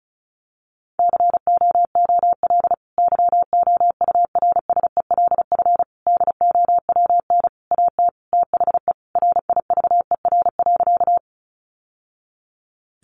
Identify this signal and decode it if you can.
Morse code, heard as a keyed tone, about 34 wpm, 700 Hz. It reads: COOL YOURSELF DOWN AT THE RIVER.
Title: Sentences from Top 1200 Words 35wpm